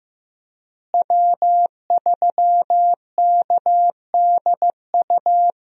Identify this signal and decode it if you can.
Morse code, heard as a keyed tone, about 15 wpm, 700 Hz. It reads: W3KDU
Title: USA Call Signs - 5 Characters - Repeated 3x - 15wpm